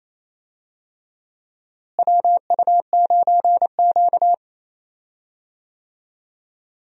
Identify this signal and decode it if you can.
Morse code, heard as a keyed tone, about 28 wpm, 700 Hz. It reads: WU9Q
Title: USA Call Signs - 4 Characters - Repeated 2x - 28wpm